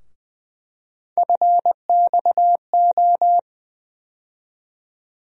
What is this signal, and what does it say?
Morse code, heard as a keyed tone, about 21 wpm, 700 Hz. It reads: FXO
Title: USA Call Sign Postfixes - 3 Letters 20wpm